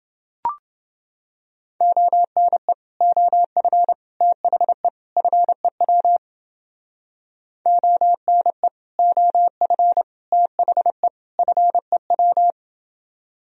Morse code, heard as a keyed tone, about 29 wpm, 700 Hz. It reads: ONE OF THE FEW ONE OF THE FEW